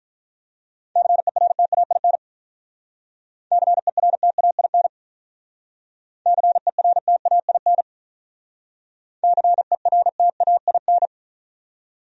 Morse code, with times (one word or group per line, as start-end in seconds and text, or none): 1.0-2.2 CERTAIN
3.5-4.9 CERTAIN
6.3-7.8 CERTAIN
9.2-11.1 CERTAIN